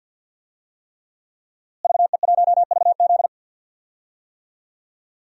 Morse code, heard as a keyed tone, about 52 wpm, 700 Hz. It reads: VE1VZ
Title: DXCC-100 Call Signs - 5 Characters - Repeated 3x - 50wpm